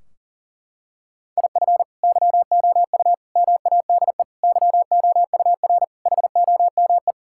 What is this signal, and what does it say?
Morse code, heard as a keyed tone, about 40 wpm, 700 Hz. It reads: IF YOU MADE YOUR HOME
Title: Sentences from Top 1600 Words 40wpm